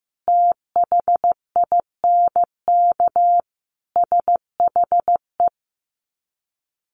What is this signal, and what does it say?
Morse code, heard as a keyed tone, about 15 wpm, 700 Hz. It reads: THINK SHE
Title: Sets of 5 Words 15wpm